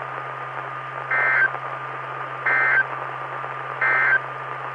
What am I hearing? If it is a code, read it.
Emergency Alert System (SAME) tones, heard as harsh buzzing bursts. It is heard from 1.0 to 4.3 s.
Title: Saline, AK SKYWARN Repeater System - W5RHS